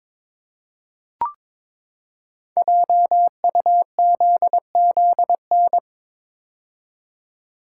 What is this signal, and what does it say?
Morse code, heard as a keyed tone, about 22 wpm, 700 Hz. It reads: JUZZN
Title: Code Groups - 5 Letters - 22wpm